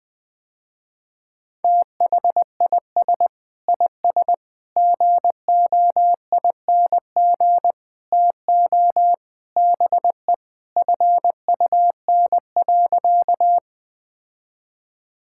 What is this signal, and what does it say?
Morse code, heard as a keyed tone, about 20 wpm, 700 Hz. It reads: THIS IS GOING TO BE FUN.
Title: Sentences with 5-Word Phrases - Spoken Beginning - 20wpm